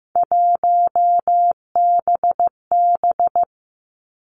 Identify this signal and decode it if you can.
Morse code, heard as a keyed tone, about 15 wpm, 700 Hz. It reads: 1BB
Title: Three Challenging Letter-Number Combinations 15wpm